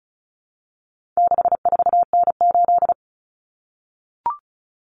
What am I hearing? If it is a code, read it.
Morse code, heard as a keyed tone, about 34 wpm, 700 Hz. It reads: 64N8
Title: Common Cut Numbers - 4-Digit 35wpm